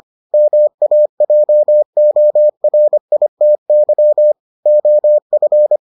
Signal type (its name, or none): Morse code